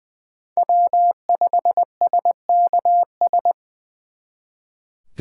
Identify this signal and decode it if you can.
Morse code, heard as a keyed tone, about 24 wpm, 700 Hz. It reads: W5SKS
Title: Speed Racing - USA Call Signs 20wpm